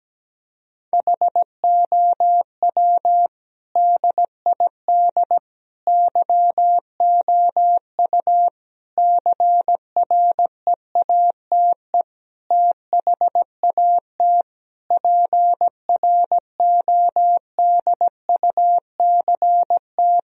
Morse code, heard as a keyed tone, about 17 wpm, 700 Hz. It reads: HOW DID YOU CREATE THAT PRODUCT
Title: Sentences from Top 1000 Words - Review Part 3 - 17wpm